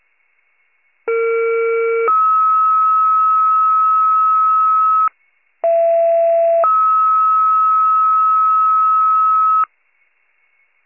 Two-tone page, heard as a one-second tone, then a three-second tone, about 460 then 1290 Hz, about 670 then 1290 Hz. It is heard from 1.1 to 9.6 s.